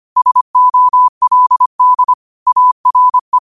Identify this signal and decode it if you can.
Morse code, heard as a keyed tone, about 25 wpm, 1000 Hz. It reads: SOLD ARE
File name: qotdfiles / 25wpm